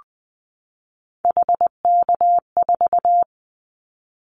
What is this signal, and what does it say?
Morse code, heard as a keyed tone, about 20 wpm, 700 Hz. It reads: HK4
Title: Callsign Prefixes for Top 100 DXCC - 20wpm